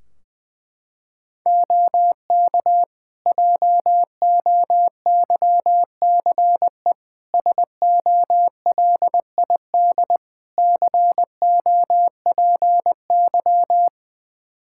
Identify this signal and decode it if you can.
Morse code, heard as a keyed tone, about 21 wpm, 700 Hz. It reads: OK JOYCE SOLID COPY
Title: Practice QSO - Exchange 3 - More Info - 20wpm